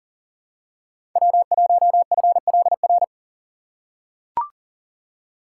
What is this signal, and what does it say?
Morse code, heard as a keyed tone, about 40 wpm, 700 Hz. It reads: W1FLR
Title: USA Call Signs - 2x Character Spacing - 40wpm